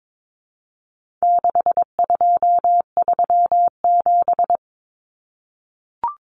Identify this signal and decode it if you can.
Morse code, heard as a keyed tone, about 22 wpm, 700 Hz. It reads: 6237
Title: Numbers - 4-Digit 22wpm